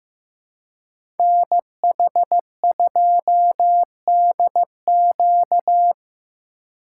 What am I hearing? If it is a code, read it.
Morse code, heard as a keyed tone, about 18 wpm, 700 Hz. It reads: NH2DQ
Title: Speed Racing - USA Call Signs 15wpm